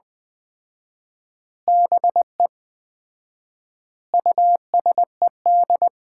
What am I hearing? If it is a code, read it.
Morse code, heard as a keyed tone, about 20 wpm, 700 Hz. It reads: BE USED
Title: 4x Word-Spacing - Sets of 4 Words - 20wpm